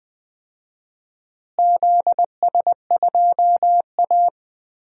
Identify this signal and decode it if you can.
Morse code, heard as a keyed tone, about 20 wpm, 700 Hz. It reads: ZS2A
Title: DXCC-100 Call Signs - 4 Characters - Repeated 2x - 20wpm